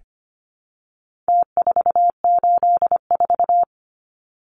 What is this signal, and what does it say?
Morse code, heard as a keyed tone, about 25 wpm, 700 Hz. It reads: T484
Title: Numbers - Contest Style - 4-Digit 25wpm